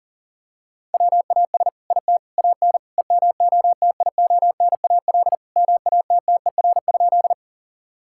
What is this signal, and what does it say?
Morse code, heard as a keyed tone, about 40 wpm, 700 Hz. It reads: WAS IT AN EMOTIONAL MATTER?
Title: Sentences from Top 1000 Words - Review Part 5 - 40wpm